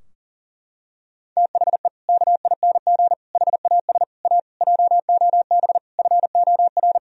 Morse code, heard as a keyed tone, about 40 wpm, 700 Hz. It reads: THE KING HAS A JOB FOR
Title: Sentences from Top 1000 Words - Review Part 3 - 40wpm